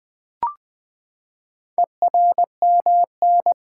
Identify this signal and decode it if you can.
Morse code, heard as a keyed tone, about 20 wpm, 700 Hz. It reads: ERMN
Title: Four Letter-Number Combinations - 20wpm